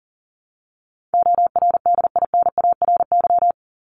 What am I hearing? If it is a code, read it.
Morse code, heard as a keyed tone, about 38 wpm, 700 Hz. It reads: ORDINARY
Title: Top 1751-2000 Words 40wpm